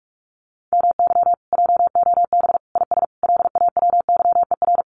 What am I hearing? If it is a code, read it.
Morse code, heard as a keyed tone, about 44 wpm, 700 Hz. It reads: MY JOB IS LAWYER
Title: Common QSO Phrases Encore 45wpm